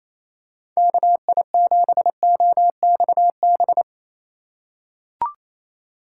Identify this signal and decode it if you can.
Morse code, heard as a keyed tone, about 28 wpm, 700 Hz. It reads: KI7OXB
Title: USA Call Signs - 6 Characters - 3x Character Spacing - 28wpm